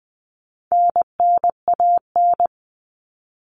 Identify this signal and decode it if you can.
Morse code, heard as a keyed tone, about 20 wpm, 700 Hz. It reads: NNAN
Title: All Cut Numbers - 4-Digit 20wpm